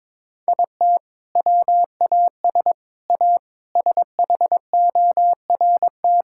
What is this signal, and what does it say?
Morse code, heard as a keyed tone, about 22 wpm, 700 Hz. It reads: IT WAS A SHORT
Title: Sentences from Top 1600 Words 22wpm